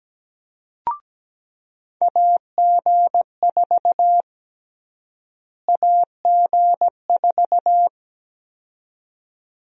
Morse code, heard as a keyed tone, about 17 wpm, 700 Hz. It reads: AG4 AG4
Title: USA Call Sign Prefixes - Repeated 2x - 17wpm